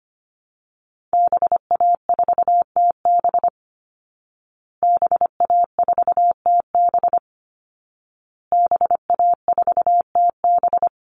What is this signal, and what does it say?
Morse code, heard as a keyed tone, about 25 wpm, 700 Hz. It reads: BA4TB BA4TB BA4TB
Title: Call Signs Repeated 2x - 25wpm